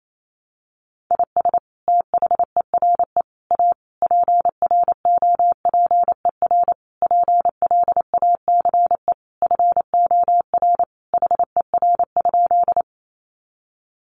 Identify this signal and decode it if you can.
Morse code, heard as a keyed tone, about 28 wpm, 700 Hz. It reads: IS THERE A PROPER PLACE FOR HER?
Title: Sentences from Top 800 Words 28wpm